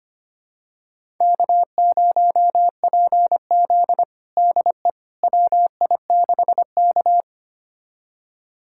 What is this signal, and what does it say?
Morse code, heard as a keyed tone, about 25 wpm, 700 Hz. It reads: K0PZ DE WI6K
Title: USA Callsign DE Callsign - 4 Characters - 25wpm